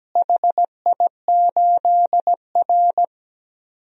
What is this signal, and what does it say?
Morse code, heard as a keyed tone, about 17 wpm, 700 Hz. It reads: HI8R
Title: DXCC-100 Call Signs - 4 Characters - Repeated 2x - 17wpm